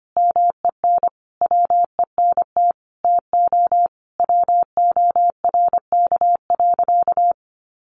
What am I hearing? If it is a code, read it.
Morse code, heard as a keyed tone, about 25 wpm, 700 Hz. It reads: MEN WENT TO WORK.
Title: Sentences from Top 1300 Words 25wpm